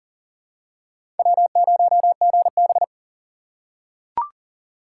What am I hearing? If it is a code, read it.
Morse code, heard as a keyed tone, about 40 wpm, 700 Hz. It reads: W0GB
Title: USA Call Signs - 4 Characters - 4x Character Spacing - 40wpm